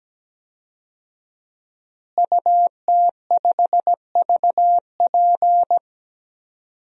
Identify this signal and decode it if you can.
Morse code, heard as a keyed tone, about 17 wpm, 700 Hz. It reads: UT5VP
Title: DXCC-100 Call Signs - 5 Characters - Repeated 3x - 17wpm